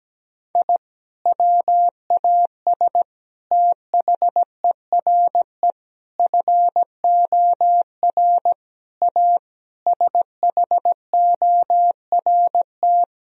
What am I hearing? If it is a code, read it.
Morse code, heard as a keyed tone, about 17 wpm, 700 Hz. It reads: I WAS THERE FOR A SHORT
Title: Sentences from Top 1000 Words - Review Part 2 - 17wpm